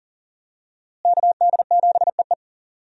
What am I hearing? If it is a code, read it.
Morse code, heard as a keyed tone, about 40 wpm, 700 Hz. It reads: KD7EE
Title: USA Call Signs - 5 Characters - Repeated 2x - 40wpm